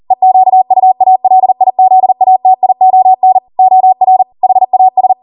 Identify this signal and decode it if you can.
Morse code, heard as a keyed tone, about 40 wpm, 750 Hz. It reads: EQUALIZATION OR HAS